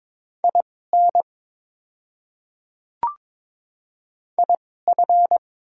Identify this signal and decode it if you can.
Morse code, heard as a keyed tone, about 22 wpm, spaced as for 16 wpm, 690 Hz. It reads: IN IF
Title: 2x Character Spacing - Top 100 Words - 22wpm